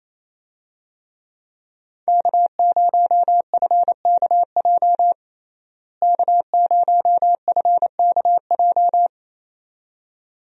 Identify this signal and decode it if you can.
Morse code, heard as a keyed tone, about 28 wpm, 700 Hz. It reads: K0FKJ K0FKJ
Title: USA Call Signs - 5 Characters - Repeated 2x - 28wpm